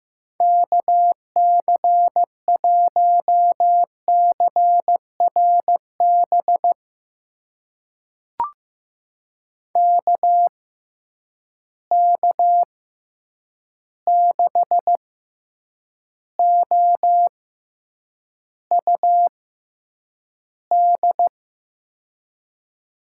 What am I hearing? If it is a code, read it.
Morse code, heard as a keyed tone, about 15 wpm, spaced as for 8 wpm, 700 Hz. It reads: KC1CRB KK6OUD
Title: USA Call Signs - 6 Characters - 6x Character Spacing - 15wpm